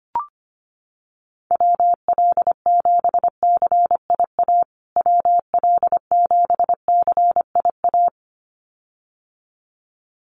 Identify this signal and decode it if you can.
Morse code, heard as a keyed tone, about 25 wpm, 700 Hz. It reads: WL7CIA WL7CIA
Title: USA Call Signs Repeated 2x 25wpm